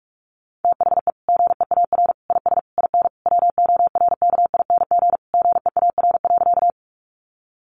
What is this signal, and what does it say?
Morse code, heard as a keyed tone, about 44 wpm, 700 Hz. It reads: THE GEAR IS IN WORKING GEAR.
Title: Sentences from Top 1400 Words 45wpm